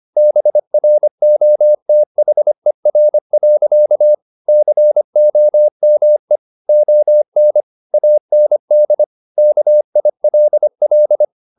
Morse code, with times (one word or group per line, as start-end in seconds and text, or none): 0.2-4.1 BROTHER.
4.5-6.4 COME
6.7-7.6 ON
7.9-9.0 AND
9.4-11.2 KILL